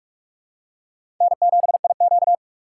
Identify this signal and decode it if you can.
Morse code, heard as a keyed tone, about 45 wpm, 700 Hz. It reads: N7IQ
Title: USA Call Signs - 2x Character Spacing - 45wpm